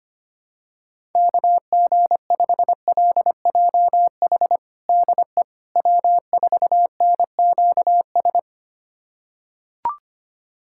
Morse code, heard as a keyed tone, about 25 wpm, 700 Hz. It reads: KG5LJH DE W4NQS
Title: USA Callsign DE Callsign 25wpm